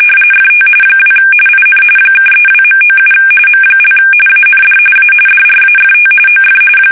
RTTY, about 45.45 baud. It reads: FT 2 IN (32.97 M) AND A DRAFT OF
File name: fsk850 / USS-Massachusetts